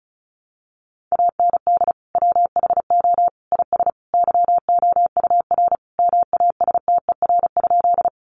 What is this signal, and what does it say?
Morse code, heard as a keyed tone, about 34 wpm, 700 Hz. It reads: AND WHO IS YOUR MASTER?
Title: Sentences from Top 1300 Words 35wpm